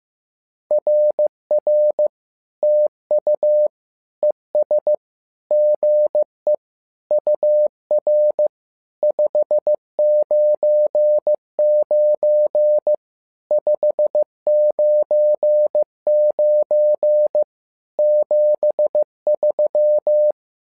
Morse code, heard as a keyed tone, about 15 wpm, 600 Hz. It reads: RR TU ES GE UR 599 599 73
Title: SOTA QSO Practice 15wpm